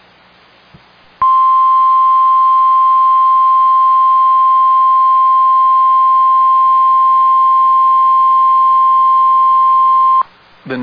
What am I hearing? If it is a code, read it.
Emergency Alert System (SAME) tones (the 1050 Hz attention tone). It is heard from 1.2 to 10.3 s.